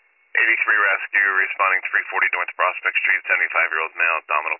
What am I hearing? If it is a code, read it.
Emergency Alert System (SAME) tones, heard as harsh buzzing bursts, with speech. It is heard from 0.5 to 3.5 s.